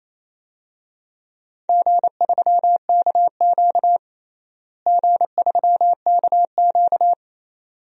Morse code, heard as a keyed tone, about 28 wpm, 700 Hz. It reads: G3KQ G3KQ
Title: DXCC-100 Call Signs - 4 Characters - Repeated 2x - 28wpm